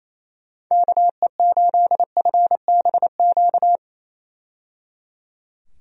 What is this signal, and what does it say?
Morse code, heard as a keyed tone, about 27 wpm, 700 Hz. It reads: KE8FBQ
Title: USA Call Signs - 6 Characters - 28wpm